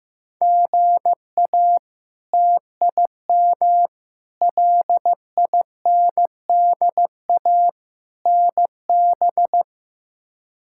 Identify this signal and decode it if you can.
Morse code, heard as a keyed tone, about 15 wpm, 700 Hz. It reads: GA TIM LINDA NB
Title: K1USN SST QSO Practice - 15wpm